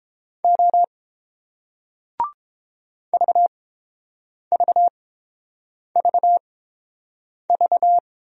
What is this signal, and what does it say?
Morse code, heard as a keyed tone, about 30 wpm, spaced as for 7 wpm, 700 Hz. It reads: O VVVV